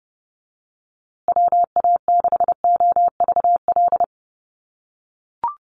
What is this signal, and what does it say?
Morse code, heard as a keyed tone, about 30 wpm, 700 Hz. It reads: WA6OVL
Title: USA Call Signs Repeated 2x 30wpm